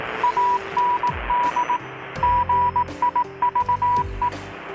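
Morse code, heard as a keyed tone, about 18 wpm, 980 Hz, with radio noise. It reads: AND GIVE
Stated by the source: Morse from KPH's Night of Nights XX maritime radiotelegraphy event, received in Chiapas, Mexico about 3,800 km away on a Kenwood R-600 with a grounded 10 m long-wire antenna.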